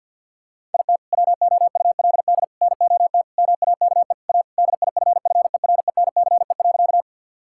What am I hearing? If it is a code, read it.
Morse code, heard as a keyed tone, about 50 wpm, 700 Hz. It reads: IT WOULD NOT MAKE A DIFFERENCE.
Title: Sentences from Top 1000 Words - Review Part 5 - 50wpm